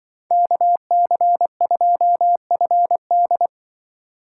Morse code, heard as a keyed tone, about 23 wpm, 700 Hz. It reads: KC2FD
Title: Speed Racing - USA 2x2 Call Signs 20wpm